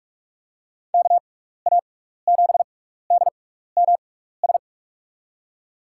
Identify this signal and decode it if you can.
Morse code, heard as a keyed tone, about 46 wpm, spaced as for 15 wpm, 700 Hz. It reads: KA7DMS
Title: USA Call Signs - 6 Characters - 6x Character Spacing - 45wpm